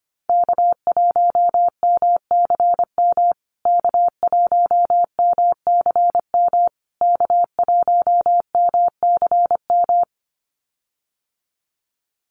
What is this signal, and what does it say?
Morse code, heard as a keyed tone, about 25 wpm, 700 Hz. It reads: K1MCM K1MCM K1MCM
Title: USA Call Signs Repeated 3x 25wpm